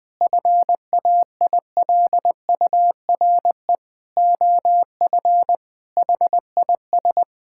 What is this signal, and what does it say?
Morse code, heard as a keyed tone, about 20 wpm, 700 Hz. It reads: FAILURE OF HIS
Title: Sentences from Top 1000 Words - Review Part 6 - 20wpm